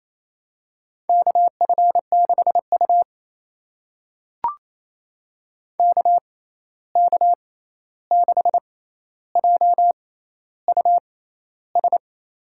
Morse code, heard as a keyed tone, about 28 wpm, spaced as for 13 wpm, 700 Hz. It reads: KF6U KK6JUS
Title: USA Call Signs - 6x Character Spacing - 28wpm